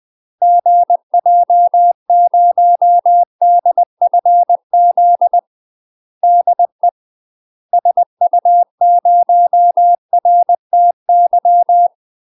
Morse code, heard as a keyed tone, about 20 wpm, 700 Hz. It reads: GJ0DFZ DE SU0RTY
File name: UK MASTER QSOs TRAINING / 20WPM / UK MASTER QSOs 0042